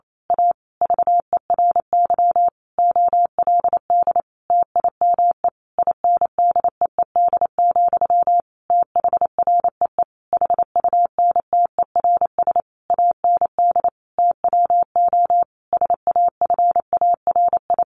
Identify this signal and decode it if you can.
Morse code, heard as a keyed tone, about 28 wpm, 700 Hz. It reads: A VERY OLD TIME INDEED, THREE HUNTERS AND TWO SAFARI
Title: A Sound Of Thunder 28wpm